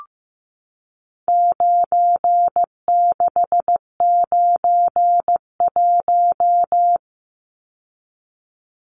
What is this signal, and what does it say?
Morse code, heard as a keyed tone, about 15 wpm, 700 Hz. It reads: 9691
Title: Numbers - 4-Digit 15wpm